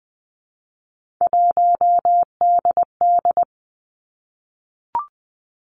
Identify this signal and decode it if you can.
Morse code, heard as a keyed tone, about 20 wpm, 700 Hz. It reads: 1DD
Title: Three Challenging Letter-Number Combinations 20wpm